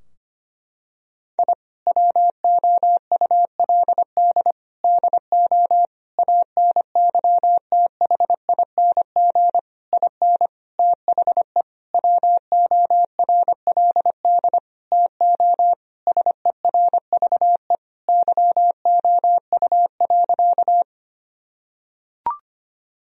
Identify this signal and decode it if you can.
Morse code, heard as a keyed tone, about 25 wpm, 700 Hz. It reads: I WOULD DO ANYTHING IN THE WORLD TO SERVE YOU.
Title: Sentences from Top 1000 Words - Review Part 2 - 25wpm